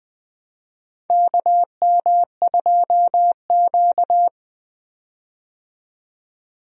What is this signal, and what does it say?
Morse code, heard as a keyed tone, about 21 wpm, 700 Hz. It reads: KM2Q
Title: USA Call Signs - 4 Characters - 20wpm